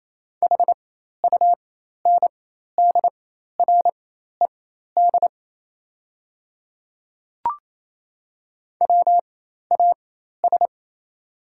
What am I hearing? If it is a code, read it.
Morse code, heard as a keyed tone, about 27 wpm, spaced as for 13 wpm, 700 Hz. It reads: HUNDRED WAS
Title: 4x Character Spacing - Top 500 Words - 28wpm